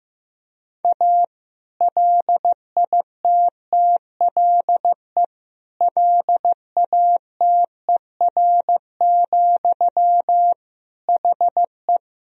Morse code, heard as a keyed tone, about 15 wpm, 700 Hz. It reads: A LITTLE LATER, HE
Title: Sentences from Top 1000 Words - Review Part 4 - 15wpm